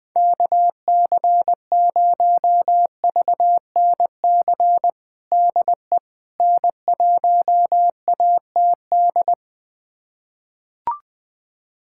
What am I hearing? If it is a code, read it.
Morse code, heard as a keyed tone, about 21 wpm, 700 Hz. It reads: KC0VNC DE N1ATD
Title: USA Callsign DE Callsign 20wpm